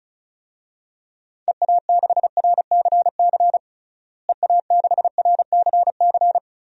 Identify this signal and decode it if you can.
Morse code, heard as a keyed tone, about 36 wpm, 700 Hz. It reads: EA6RCC EA6RCC
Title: DXCC-100 Call Signs - 6 Characters - Repeated 3x - 35wpm